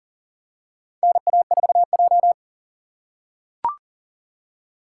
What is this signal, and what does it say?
Morse code, heard as a keyed tone, about 40 wpm, 700 Hz. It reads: NA4J
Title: USA Call Signs - 4 Characters - 2x Character Spacing - 40wpm